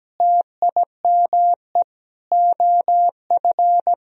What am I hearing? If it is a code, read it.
Morse code, heard as a keyed tone, about 17 wpm, 700 Hz. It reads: TIME OF
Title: Sentences from Top 300 Words and Abbreviations 17wpm